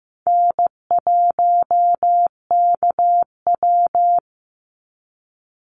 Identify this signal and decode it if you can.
Morse code, heard as a keyed tone, about 15 wpm, 700 Hz. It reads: N1KW
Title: Easy Call Signs Repeated 3x - 15wpm